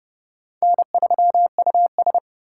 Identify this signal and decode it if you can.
Morse code, heard as a keyed tone, about 38 wpm, 700 Hz. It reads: N3US
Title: Speed Racing - USA 1x2 Call Signs 30wpm